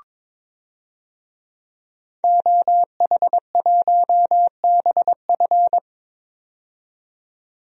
Morse code, heard as a keyed tone, about 22 wpm, 700 Hz. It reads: OH1BF
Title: DXCC-100 Call Signs - 5 Characters - 22wpm